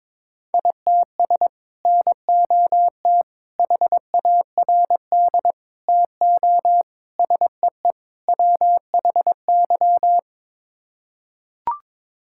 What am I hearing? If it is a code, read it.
Morse code, heard as a keyed tone, about 22 wpm, 700 Hz. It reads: ITS NOT HARD TO SEE WHY
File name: practice / 22wpm / Sets of 6 Words - 22wpm